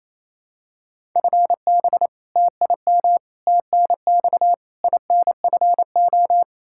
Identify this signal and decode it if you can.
Morse code, heard as a keyed tone, about 28 wpm, 700 Hz. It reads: FB TIM TNX INFO
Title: Practice QSO - Exchange 3 - More Info - 28wpm